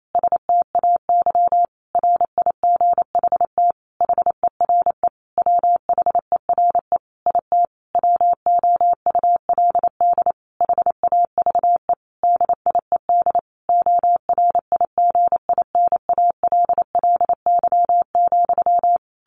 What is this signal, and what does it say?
Morse code, heard as a keyed tone, about 28 wpm, 700 Hz. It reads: STAY RIGHT HERE WHERE IT WOULD HAVE DIED ORIGINALLY,
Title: A Sound Of Thunder 28wpm